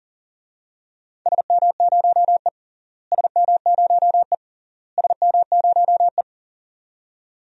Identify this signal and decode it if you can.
Morse code, heard as a keyed tone, about 37 wpm, 700 Hz. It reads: SM0E SM0E SM0E
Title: DXCC-100 Call Signs - 4 Characters - Repeated 3x - 40wpm